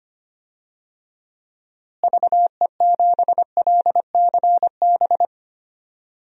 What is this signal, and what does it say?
Morse code, heard as a keyed tone, about 25 wpm, 700 Hz. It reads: VE7LCB
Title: DXCC-100 Call Signs - 6 Characters - Repeated 3x - 25wpm